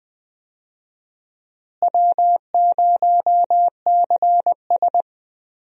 Morse code, heard as a keyed tone, about 20 wpm, 700 Hz. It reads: W0CS
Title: USA Call Signs - 4 Characters - Repeated 2x - 20wpm